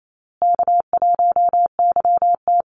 Morse code, heard as a keyed tone, about 28 wpm, 700 Hz. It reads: K1YT
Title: Easy Call Signs Repeated 3x - 28wpm